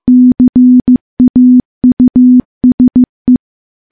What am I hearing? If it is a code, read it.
Morse code, heard as a keyed tone, about 15 wpm, 250 Hz. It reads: CAUSE
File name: morse-training-15WPM